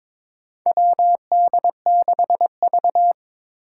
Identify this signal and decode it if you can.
Morse code, heard as a keyed tone, about 23 wpm, 700 Hz. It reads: WD6V
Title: Four Challenging Letter-Number Combinations 22wpm